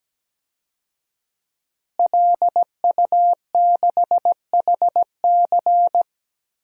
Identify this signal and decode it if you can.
Morse code, heard as a keyed tone, about 17 wpm, 700 Hz. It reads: LU6HC
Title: DXCC-100 Call Signs - 5 Characters - Repeated 2x - 17wpm